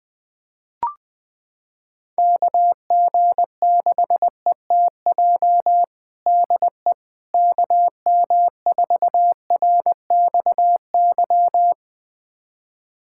Morse code, heard as a keyed tone, about 20 wpm, 700 Hz. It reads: KG6ETJ DE KM4RXY
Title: USA Callsign DE Callsign 20wpm